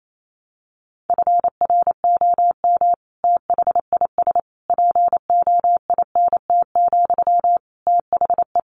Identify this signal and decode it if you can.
Morse code, heard as a keyed tone, about 28 wpm, 700 Hz. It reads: FROM THIS POINT, THE
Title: Sentences from Top 1200 Words 28wpm